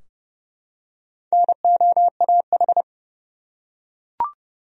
Morse code, heard as a keyed tone, about 30 wpm, 700 Hz. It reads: NOAH
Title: USA Common Names - Max 4 Letters - 30wpm